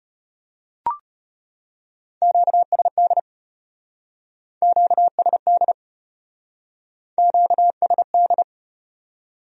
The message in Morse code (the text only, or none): QSD QSD QSD